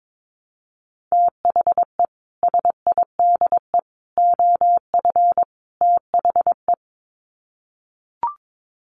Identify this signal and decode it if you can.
Morse code, heard as a keyed tone, about 22 wpm, 700 Hz. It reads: THE SIDE OF THE
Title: Sets of 4 Words 22wpm